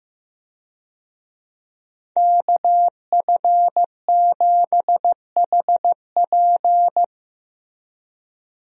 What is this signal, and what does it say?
Morse code, heard as a keyed tone, about 16 wpm, 700 Hz. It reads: KF7HP
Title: USA Call Signs - 5 Characters - Repeated 3x - 15wpm